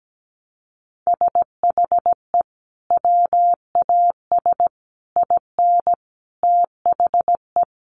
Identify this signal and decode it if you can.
Morse code, heard as a keyed tone, about 17 wpm, 700 Hz. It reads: SHE WAS IN THE
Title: Sentences from Top 600 Words 17wpm